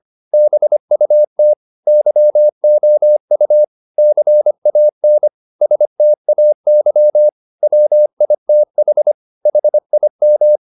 Morse code, heard as a keyed tone, about 25 wpm, 600 Hz. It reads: BUT YOU CAN STAY WITH HIM
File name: the old man and the sea part 1 25wpm